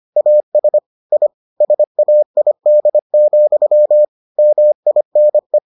Morse code, heard as a keyed tone, about 25 wpm, 600 Hz. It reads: AS I SAID, MINE